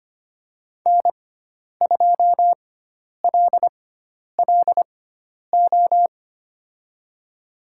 Morse code, heard as a keyed tone, about 25 wpm, spaced as for 10 wpm, 700 Hz. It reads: N2LLO
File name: course / 25:10wpm / 095 Lesson 16g - Letter c - Call Signs 25wpm